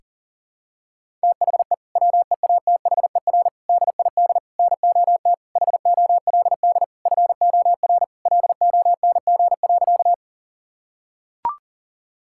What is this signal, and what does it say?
Morse code, heard as a keyed tone, about 40 wpm, 700 Hz. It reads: THE WEATHER DID NOT HOLD FOR LONG.
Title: Sentences from Top 1000 Words - Review Part 5 - 40wpm